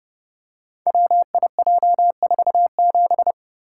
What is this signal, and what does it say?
Morse code, heard as a keyed tone, about 30 wpm, 700 Hz. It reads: WIJ47
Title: Code Groups - 5 Letter:Numbers - 30wpm